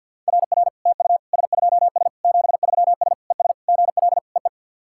Morse code, heard as a keyed tone, about 46 wpm, 690 Hz. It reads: RR TU S2S 73S ES GL EE
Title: SOTA QSO Practice 50wpm